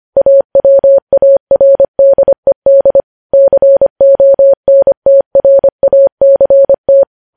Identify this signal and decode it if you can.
Morse code, heard as a keyed tone, about 25 wpm, 550 Hz. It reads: AWARDED CONTRACT
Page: news-headlines-in-morse-code-at-25-wpm